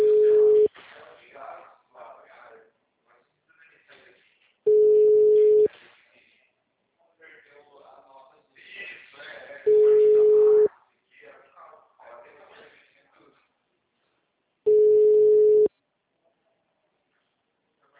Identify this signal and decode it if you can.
Ringback tone (Europe, 425 Hz), with speech.